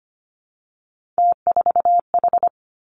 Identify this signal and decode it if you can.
Morse code, heard as a keyed tone, about 25 wpm, 700 Hz. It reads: T4H
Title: Three Challenging Letter-Number Combinations 25wpm